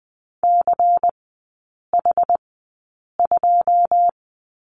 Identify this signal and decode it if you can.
Morse code, heard as a keyed tone, about 20 wpm, spaced as for 8 wpm, 700 Hz. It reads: CH2
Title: 093 Lesson 16e - Letter c - Three Characters 20wpm